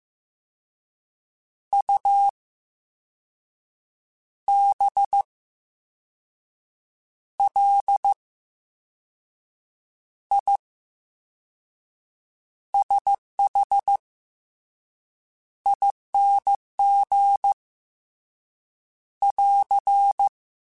Morse code, heard as a keyed tone, about 15 wpm, spaced as for 5 wpm, 800 Hz. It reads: UBLISHING <AR>
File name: Special INGs-mp3s / Special INGs-mp3s / VerySlowToFaster Ing @13